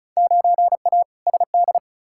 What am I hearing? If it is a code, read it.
Morse code, heard as a keyed tone, about 35 wpm, 700 Hz. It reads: 9A SD
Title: Field Day Exchanges Encore - 35wpm